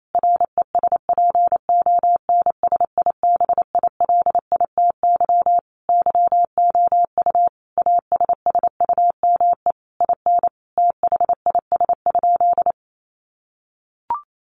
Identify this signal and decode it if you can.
Morse code, heard as a keyed tone, about 28 wpm, 700 Hz. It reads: RESPONSIBILITY YOU ASSUME IN THIS?
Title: Sentences from Top 1500 Words 28wpm